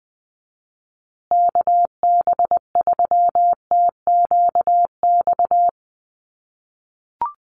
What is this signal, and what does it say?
Morse code, heard as a keyed tone, about 20 wpm, 700 Hz. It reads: KB3TQX
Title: USA Call Signs Repeated 2x 20wpm